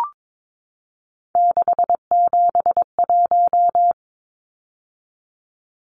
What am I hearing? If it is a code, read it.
Morse code, heard as a keyed tone, about 22 wpm, 700 Hz. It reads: 671
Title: Numbers - 3-Digit 22wpm